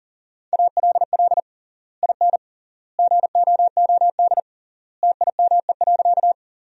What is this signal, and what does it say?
Morse code, heard as a keyed tone, about 40 wpm, 700 Hz. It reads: ALL IN GOOD TIME.
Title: 3x Word-Spacing - Sentences from Top 200 Words - 40wpm